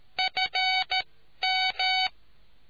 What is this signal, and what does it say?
Morse code, heard as a keyed tone, about 13 wpm, spaced as for 11 wpm, 750 Hz. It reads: FM